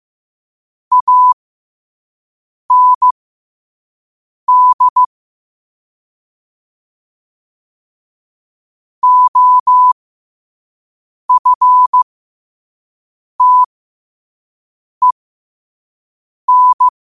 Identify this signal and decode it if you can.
Morse code, heard as a keyed tone, about 15 wpm, spaced as for 5 wpm, 1000 Hz. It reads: AND OFTEN
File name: qotdfiles / 5wpm